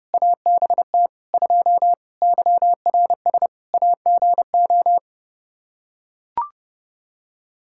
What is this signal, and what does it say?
Morse code, heard as a keyed tone, about 32 wpm, 700 Hz. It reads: ABT 2 YRS AGO
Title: Practice QSO - Exchange 3 - More Info - 30wpm